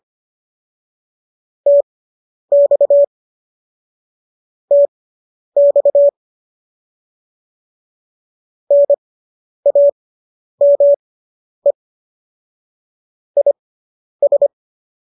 Morse code, heard as a keyed tone, about 25 wpm, spaced as for 10 wpm, 570 Hz. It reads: TX TX NAME IS